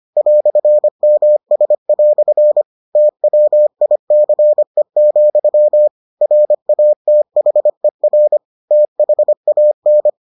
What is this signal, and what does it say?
Morse code, heard as a keyed tone, about 25 wpm, 600 Hz. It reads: "MS" TWICE, RATHER THAN